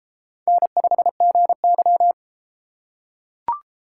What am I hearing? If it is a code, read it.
Morse code, heard as a keyed tone, about 31 wpm, 700 Hz. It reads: N5GY